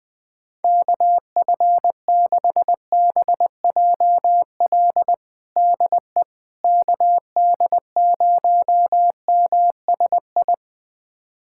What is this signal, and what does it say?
Morse code, heard as a keyed tone, about 21 wpm, 700 Hz. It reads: KF6BJL DE KD0MSI
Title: USA Callsign DE Callsign - 6 Characters - 20wpm